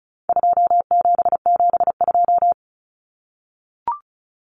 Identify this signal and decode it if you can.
Morse code, heard as a keyed tone, about 34 wpm, 700 Hz. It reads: 2772
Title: Common Cut Numbers - 4-Digit 35wpm